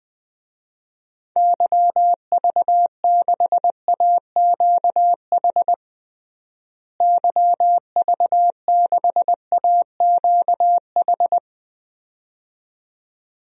Morse code, heard as a keyed tone, about 20 wpm, 700 Hz. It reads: YV6AQH YV6AQH
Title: DXCC-100 Call Signs - 6 Characters - Repeated 2x - 20wpm